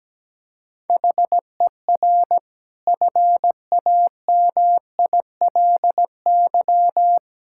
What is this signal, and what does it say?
Morse code, heard as a keyed tone, about 17 wpm, 700 Hz. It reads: HER FAMILY
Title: Sentences from Top 1000 Words - Review Part 6 - 17wpm